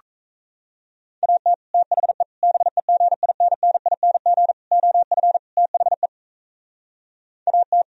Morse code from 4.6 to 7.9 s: OF THE AT

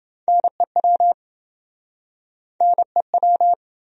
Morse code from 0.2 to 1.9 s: NEW

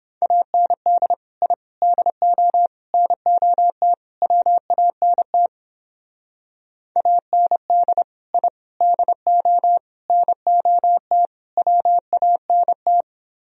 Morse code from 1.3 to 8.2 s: I DO NOT WANT AND

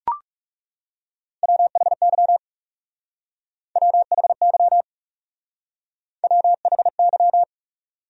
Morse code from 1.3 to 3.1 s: WHY

